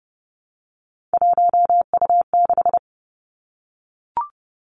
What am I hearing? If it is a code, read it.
Morse code, heard as a keyed tone, about 30 wpm, 700 Hz. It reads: 1U6
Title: Three Challenging Letter-Number Combinations 30wpm